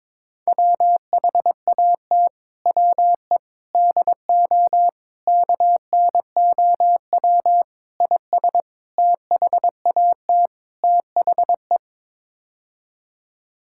Morse code, heard as a keyed tone, about 22 wpm, 700 Hz. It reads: WHAT WE DO KNOW IS THAT THE
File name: practice / 22wpm / Sets of 7 Words - 22wpm